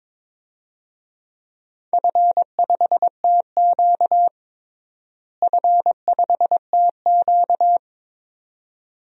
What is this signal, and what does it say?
Morse code, heard as a keyed tone, about 22 wpm, 700 Hz. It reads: F5TQ F5TQ
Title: DXCC-100 Call Signs - 4 Characters - Repeated 2x - 22wpm